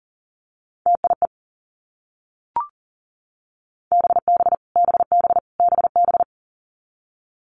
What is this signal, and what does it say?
Morse code, heard as a keyed tone, about 40 wpm, 700 Hz. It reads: TIE BB BB BB
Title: Multi-letter Phonemes Repeated 3x 40wpm